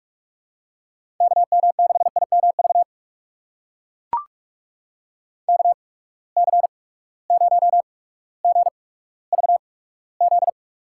Morse code, heard as a keyed tone, about 43 wpm, spaced as for 19 wpm, 700 Hz. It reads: KM6IMV KC0GVZ
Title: USA Call Signs - 6 Characters - 8x Character Spacing - 45wpm